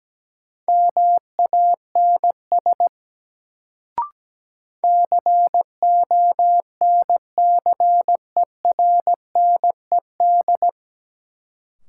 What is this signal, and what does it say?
Morse code, heard as a keyed tone, about 17 wpm, 700 Hz. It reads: MANS CONCERNED
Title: Sentences from Top 1000 Words - Review Part 6 - New Word Drill - 17wpm